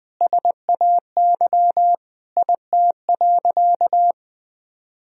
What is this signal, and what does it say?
Morse code, heard as a keyed tone, about 20 wpm, 700 Hz. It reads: SAY IT.
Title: Sentences from Top 400 Words and Abbreviations 20wpm